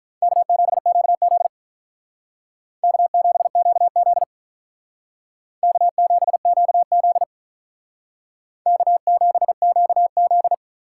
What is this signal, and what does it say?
Morse code, heard as a keyed tone, about 42 wpm, 700 Hz. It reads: K7QZ K7QZ K7QZ K7QZ
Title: Speed Racing - USA 1x2 Call Signs 35wpm